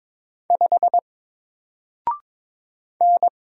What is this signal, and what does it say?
Morse code, heard as a keyed tone, about 22 wpm, 700 Hz. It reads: 5 N
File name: Single-Letter-Number-22wpm